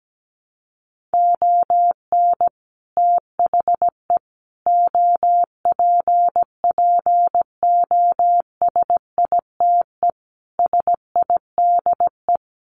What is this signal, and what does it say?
Morse code, heard as a keyed tone, about 17 wpm, 700 Hz. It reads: ON THE OPPOSITE SIDE
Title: Sentences from Top 1400 Words 17wpm